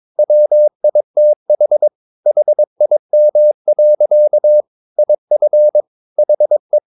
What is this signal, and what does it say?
Morse code, heard as a keyed tone, about 22 wpm, 600 Hz. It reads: WITH HIM. IF HE
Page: Romanen Never af Ken Follet på engelsk.